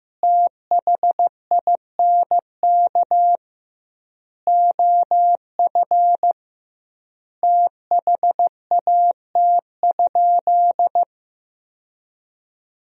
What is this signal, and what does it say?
Morse code, heard as a keyed tone, about 15 wpm, 700 Hz. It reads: THINK OF THAT?
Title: 2x Word-Spacing - Sentences from Top 200 Words - 15wpm